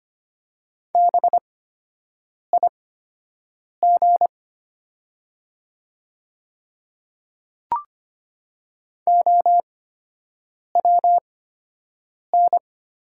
Morse code, heard as a keyed tone, about 25 wpm, spaced as for 7 wpm, 690 Hz. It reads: BIG OWN
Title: 8x Character Spacing - Top 300 Words - 25wpm